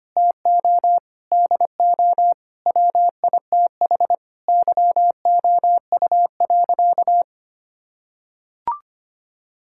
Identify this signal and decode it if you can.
Morse code, heard as a keyed tone, about 25 wpm, 700 Hz. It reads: TO DO WITH YOU.
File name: practice / 25wpm / Sentences with 4-Word Phrases - 25wpm